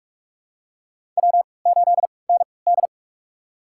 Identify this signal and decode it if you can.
Morse code, heard as a keyed tone, about 43 wpm, spaced as for 25 wpm, 690 Hz. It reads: W8ND
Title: USA Call Signs - 4 Characters - 3x Character Spacing - 45wpm